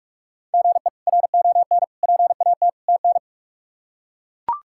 Morse code, heard as a keyed tone, about 42 wpm, 700 Hz. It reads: GE RON PAT TN
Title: K1USN SST QSO Practice - 45wpm